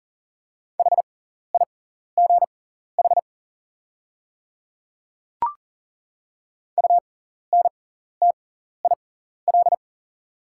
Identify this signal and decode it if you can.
Morse code, heard as a keyed tone, about 40 wpm, spaced as for 14 wpm, 690 Hz. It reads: HIGH UNTIL
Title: 6x Character Spacing - Top 500 Words - 40wpm